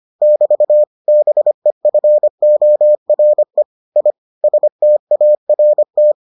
Morse code, heard as a keyed tone, about 25 wpm, 600 Hz. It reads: <BT> BEFORE I START